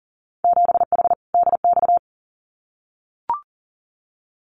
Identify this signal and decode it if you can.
Morse code, heard as a keyed tone, about 40 wpm, 700 Hz. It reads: 7H DX
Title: Winter Field Day Exchanges Encore 40wpm